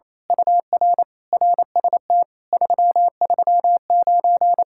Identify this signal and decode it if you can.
Morse code, heard as a keyed tone, about 27 wpm, 700 Hz. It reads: UR RST 339
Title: Speed-Racing Common QSO Phrases - 28wpm